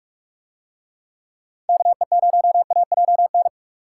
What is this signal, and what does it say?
Morse code, heard as a keyed tone, about 44 wpm, 700 Hz. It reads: KE0AJN